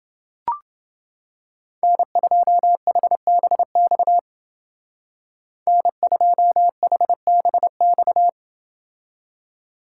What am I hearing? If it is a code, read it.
Morse code, heard as a keyed tone, about 30 wpm, 700 Hz. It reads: N2HBX N2HBX